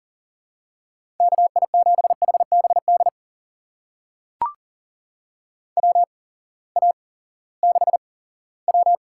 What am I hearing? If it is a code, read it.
Morse code, heard as a keyed tone, about 37 wpm, spaced as for 19 wpm, 700 Hz. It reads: KI7HBD WA6W
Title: USA Call Signs - 8x Character Spacing - 40wpm